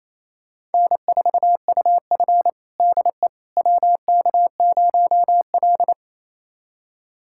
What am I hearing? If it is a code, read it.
Morse code, heard as a keyed tone, about 28 wpm, 700 Hz. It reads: N4UF DE WK0L
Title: USA Callsign DE Callsign - 4 Characters - 28wpm